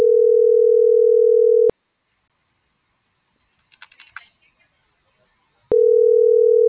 Ringback tone (North America), 440 + 480 Hz.